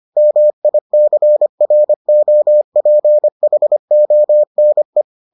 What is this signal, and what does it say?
Morse code, heard as a keyed tone, about 25 wpm, 600 Hz. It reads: MICROPHONE